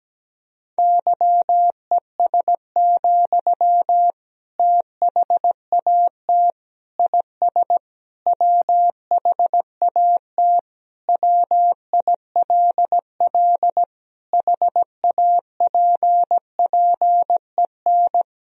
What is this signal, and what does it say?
Morse code, heard as a keyed tone, about 17 wpm, 700 Hz. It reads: YES, THAT IS WHAT WILL HAPPEN
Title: Sentences from Top 500 Words and Abbreviations 17wpm